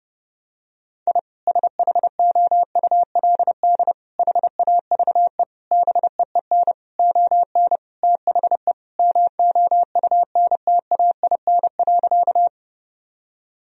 Morse code, heard as a keyed tone, about 32 wpm, 700 Hz. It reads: I SHOULD HAVE BEEN ON THE MOUNTAIN.
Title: Sentences from Top 1000 Words - Review Part 1 - 30wpm